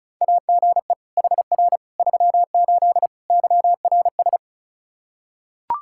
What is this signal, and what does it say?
Morse code, heard as a keyed tone, about 36 wpm, 700 Hz. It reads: AGE HR 38 YRS
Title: Practice QSO - Exchange 3 - More Info - 35wpm